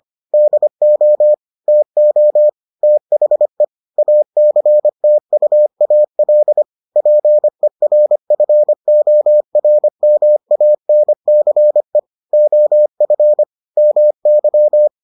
Morse code, heard as a keyed tone, about 25 wpm, 600 Hz. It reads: DO TO THE ACTUAL PERFORMANCE OF MY